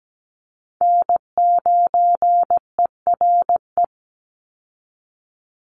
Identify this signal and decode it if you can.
Morse code, heard as a keyed tone, about 17 wpm, 700 Hz. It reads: N9ERE
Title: USA Call Signs 17wpm